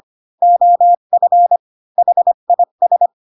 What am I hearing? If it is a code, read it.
Morse code, heard as a keyed tone, about 25 wpm, 700 Hz. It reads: OF HIS
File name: Pinocchio / Pinocchio ch5 20 25 ik1hge